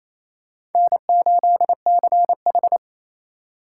Morse code, heard as a keyed tone, about 28 wpm, 700 Hz. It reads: N8CH
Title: USA Call Signs - 4 Characters - 3x Character Spacing - 28wpm